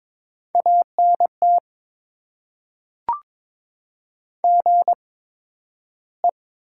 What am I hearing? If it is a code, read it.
Morse code, heard as a keyed tone, about 21 wpm, spaced as for 12 wpm, 700 Hz. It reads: ANT GE
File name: Top 20 QSO Elements - 8x Character Spacing - 22wpm